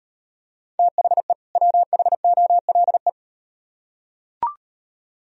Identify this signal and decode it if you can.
Morse code, heard as a keyed tone, about 38 wpm, 700 Hz. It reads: THE WHOLE